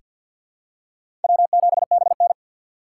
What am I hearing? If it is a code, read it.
Morse code, heard as a keyed tone, about 49 wpm, 700 Hz. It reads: W7BN
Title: USA Call Signs - 3x Character Spacing - 50wpm